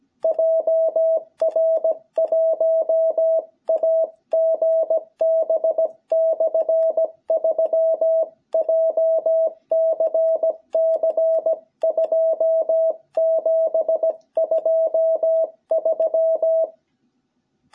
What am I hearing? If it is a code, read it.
Morse code, heard as a keyed tone, about 17 wpm, 660 Hz. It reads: JR1AGB/3 JCC 2723
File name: jh8cbh / 2021cw